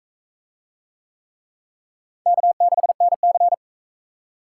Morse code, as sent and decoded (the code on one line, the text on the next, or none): -.- -.... -. -.-.
K6NC